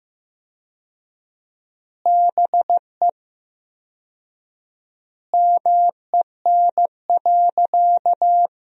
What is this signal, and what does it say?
Morse code, heard as a keyed tone, about 15 wpm, 700 Hz. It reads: BE MEN.
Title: 4x Word-Spacing - Sentences from Top 300 Words - 15wpm